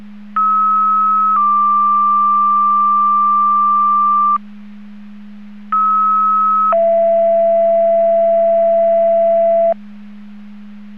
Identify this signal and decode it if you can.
Two-tone page, heard as a one-second tone, then a three-second tone, about 1290 then 1180 Hz, about 1290 then 670 Hz. It is heard from 0.4 to 9.7 s.